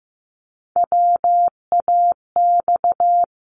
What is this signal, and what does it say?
Morse code, heard as a keyed tone, about 15 wpm, 700 Hz. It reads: WAX
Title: Multi-letter Phonemes 15wpm